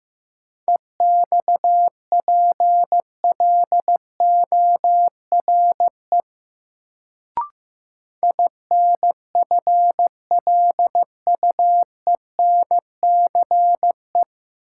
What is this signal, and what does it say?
Morse code, heard as a keyed tone, about 16 wpm, 700 Hz. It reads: EXPLORE INFLUENCE
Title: Sentences from Top 1100 Words - New Word Drill Alt - 15wpm